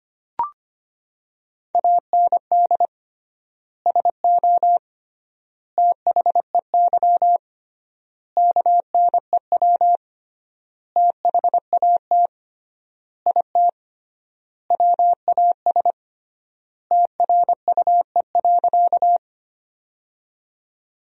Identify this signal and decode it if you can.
Morse code, heard as a keyed tone, about 25 wpm, 700 Hz. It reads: AND SO THEY KNEW THAT IT WAS TRUE.
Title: 3x Word-Spacing - Sentences from Top 400 Words - 25wpm